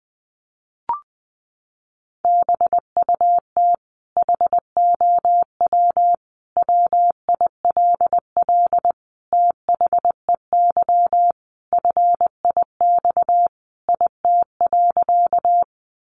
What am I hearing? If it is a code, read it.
Morse code, heard as a keyed tone, about 20 wpm, 700 Hz. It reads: BUT HOW WILL THEY FIX IT.
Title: Sentences from Top 1400 Words 20wpm